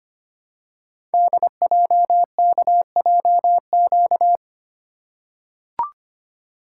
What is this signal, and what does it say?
Morse code, heard as a keyed tone, about 25 wpm, 700 Hz. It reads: DJKJQ
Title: Code Groups - 5 Letters - 25wpm